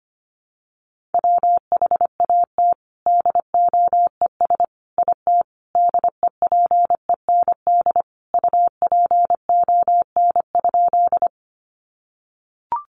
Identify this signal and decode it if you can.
Morse code, heard as a keyed tone, about 25 wpm, 700 Hz. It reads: WHAT DOES IT DEPEND UPON?
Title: Sentences from Top 1300 Words 25wpm